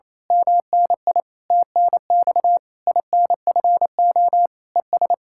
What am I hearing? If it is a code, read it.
Morse code, heard as a keyed tone, about 27 wpm, 700 Hz. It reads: MNI TNX INFO ES
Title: Speed-Racing Common QSO Phrases - 28wpm